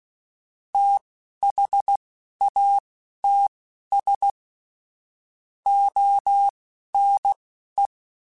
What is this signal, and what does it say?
Morse code, heard as a keyed tone, about 16 wpm, spaced as for 12 wpm, 800 Hz. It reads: THATS ONE
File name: QSOs / QSOs-mp3s / WD0FBT 1@10